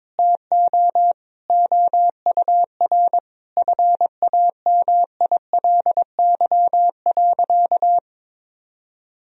Morse code, heard as a keyed tone, about 22 wpm, 700 Hz. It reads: TO OUR FAMILY.
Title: Sentences from Top 1000 Words - Review Part 4 - 22wpm